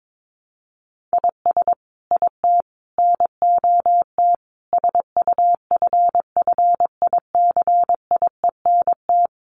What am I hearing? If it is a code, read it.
Morse code, heard as a keyed tone, about 22 wpm, 700 Hz. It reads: IS IT NOT SUFFICIENT
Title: Sentences from Top 1400 Words 22wpm